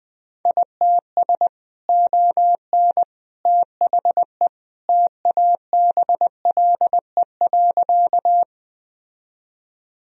Morse code, heard as a keyed tone, about 20 wpm, 700 Hz. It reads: ITS ON THE TABLE.
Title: Sentences from Top 1000 Words - Review Part 2 - 20wpm